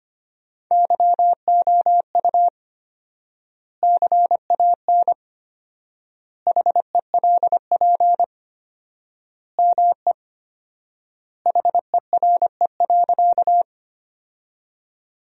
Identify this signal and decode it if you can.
Morse code, heard as a keyed tone, about 25 wpm, 700 Hz. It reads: YOU CAN HELP ME HERE.
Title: 4x Word-Spacing - Sentences from Top 200 Words - 25wpm